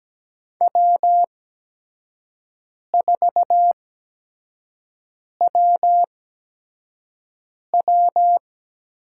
Morse code, heard as a keyed tone, about 17 wpm, spaced as for 5 wpm, 700 Hz. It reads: W4WW
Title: USA Call Signs - 4 Characters - 8x Character Spacing - 17wpm